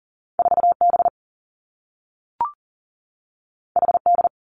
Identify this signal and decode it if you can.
Morse code, heard as a keyed tone, about 42 wpm, 700 Hz. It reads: 4 B H D
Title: Two Challenging Letter-Number Combinations 40wpm